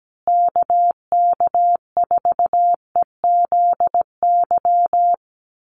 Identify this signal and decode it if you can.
Morse code, heard as a keyed tone, about 17 wpm, 700 Hz. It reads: KK4EZY
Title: USA Call Signs Repeated 2x 17wpm